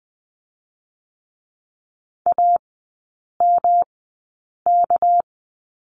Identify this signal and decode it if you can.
Morse code, heard as a keyed tone, about 20 wpm, spaced as for 8 wpm, 690 Hz. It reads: AMK
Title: 205 Lesson 33e - Letter j - Three Characters 20wpm